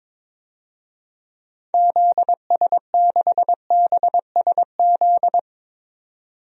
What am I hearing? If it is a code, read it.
Morse code, heard as a keyed tone, about 22 wpm, 700 Hz. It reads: ZS6BSZ
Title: DXCC-100 Call Signs - 6 Characters - 22wpm